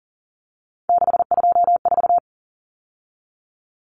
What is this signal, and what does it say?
Morse code, heard as a keyed tone, about 38 wpm, 700 Hz. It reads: 624
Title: Numbers - 3-Digit 40wpm